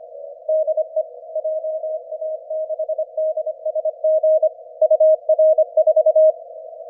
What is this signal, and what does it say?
Morse code, heard as a keyed tone, about 28 wpm, 620 Hz. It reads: DE JA6DSG UR4
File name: ja6dsg-g8bcg